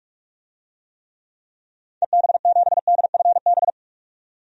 Morse code, heard as a keyed tone, about 46 wpm, 700 Hz. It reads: EB7DFB
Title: DXCC-100 Call Signs - 6 Characters - 45wpm